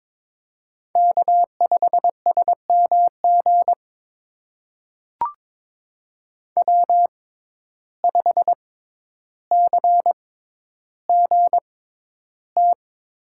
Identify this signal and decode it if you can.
Morse code, heard as a keyed tone, about 22 wpm, spaced as for 11 wpm, 700 Hz. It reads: K5SMG W5CGT
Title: USA Call Signs - 5 Characters - 6x Character Spacing - 22wpm